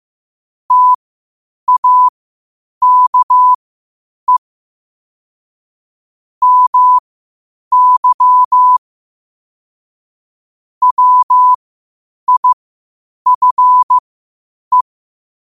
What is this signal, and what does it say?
Morse code, heard as a keyed tone, about 15 wpm, spaced as for 8 wpm, 1000 Hz. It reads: TAKE MY WIFE
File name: qotdfiles / 8wpm